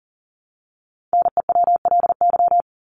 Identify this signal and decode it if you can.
Morse code, heard as a keyed tone, about 40 wpm, 700 Hz. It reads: NEWLY
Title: LY Suffix - Top 100 Words with Base Words 40wpm